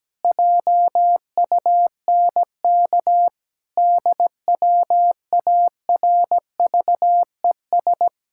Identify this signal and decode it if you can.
Morse code, heard as a keyed tone, about 17 wpm, 700 Hz. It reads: JUNK DWARVES
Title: Pangram Challenge v2 - 17wpm